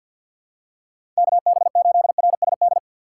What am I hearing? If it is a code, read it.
Morse code, heard as a keyed tone, about 52 wpm, 700 Hz. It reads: KB8RSD
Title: USA Call Signs - 3x Character Spacing - 50wpm